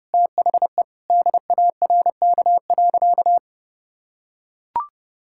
Morse code, heard as a keyed tone, about 32 wpm, 700 Hz. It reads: THE DARK.
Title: Sentences from Top 1600 Words 30wpm